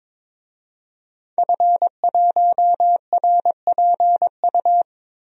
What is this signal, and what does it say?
Morse code, heard as a keyed tone, about 22 wpm, 700 Hz. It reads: F1RPU
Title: DXCC-100 Call Signs - 5 Characters - Repeated 2x - 22wpm